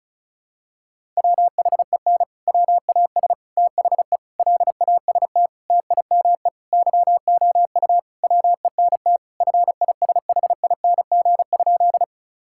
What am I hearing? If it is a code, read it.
Morse code, heard as a keyed tone, about 34 wpm, 700 Hz. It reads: WHEN WAS THE LAST TIME YOU WENT FISHING?
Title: Sentences from Top 1000 Words - Review Part 5 - 35wpm